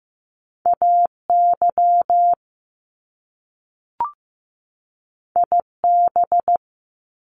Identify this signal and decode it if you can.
Morse code, heard as a keyed tone, about 15 wpm, 700 Hz. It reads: AY IB
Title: Top Two Letter Combinations in Words 15wpm